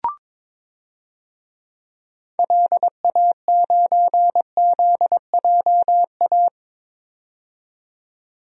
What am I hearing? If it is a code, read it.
Morse code, heard as a keyed tone, about 22 wpm, 700 Hz. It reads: LA9ZJA
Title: DXCC-100 Call Signs - 6 Characters - 22wpm